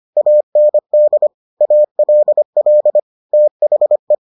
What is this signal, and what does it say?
Morse code, heard as a keyed tone, about 25 wpm, 600 Hz. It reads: AND ALL THE